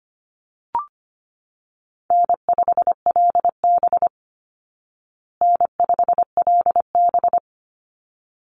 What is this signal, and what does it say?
Morse code, heard as a keyed tone, about 25 wpm, 700 Hz. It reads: N5LB N5LB
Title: Easy Call Signs Repeated 2x - 25wpm